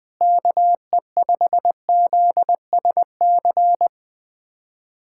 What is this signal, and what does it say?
Morse code, heard as a keyed tone, about 20 wpm, 700 Hz. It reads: KE5ZSC
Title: USA Callsign DE Callsign 20wpm